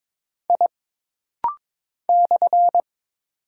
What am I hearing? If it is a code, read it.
Morse code, heard as a keyed tone, about 22 wpm, 700 Hz. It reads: I /
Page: This rapid-fire timing is geared to push your ICR (Instant Character Recognition) skills.